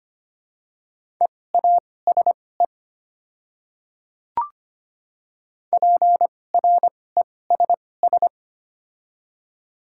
Morse code, heard as a keyed tone, about 25 wpm, spaced as for 18 wpm, 700 Hz. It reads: EASE PRESS
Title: 2x Character Spacing - Top 500 Words - 25wpm